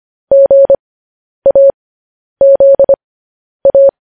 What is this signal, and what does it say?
Morse code, heard as a keyed tone, about 25 wpm, spaced as for 10 wpm, 550 Hz. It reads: GAZA